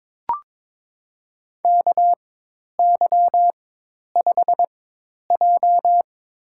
Morse code, heard as a keyed tone, about 21 wpm, spaced as for 10 wpm, 700 Hz. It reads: KY5J
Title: USA Call Signs - 4 Characters - 4x Character Spacing - 22wpm